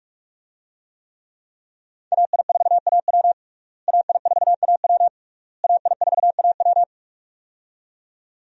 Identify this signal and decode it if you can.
Morse code, heard as a keyed tone, about 47 wpm, 700 Hz. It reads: AI4AW AI4AW AI4AW
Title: USA Call Signs - 5 Characters - Repeated 3x - 45wpm